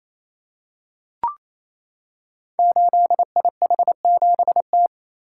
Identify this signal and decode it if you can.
Morse code, heard as a keyed tone, about 27 wpm, 700 Hz. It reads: 8IH7T
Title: Code Groups - 5 Letter:Numbers - 28wpm